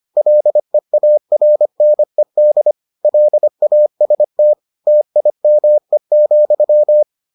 Morse code, heard as a keyed tone, about 25 wpm, 600 Hz. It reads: LEARNED LAST TIME,